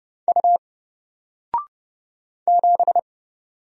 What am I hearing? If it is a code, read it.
Morse code, heard as a keyed tone, about 30 wpm, 700 Hz. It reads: U 7
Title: Single Character (Letter:Number:Punctuation:Prosign) 30wpm